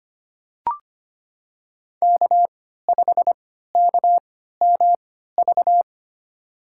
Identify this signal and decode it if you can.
Morse code, heard as a keyed tone, about 25 wpm, spaced as for 14 wpm, 700 Hz. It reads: K5KMV
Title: USA Call Signs - 3x Character Spacing - 25wpm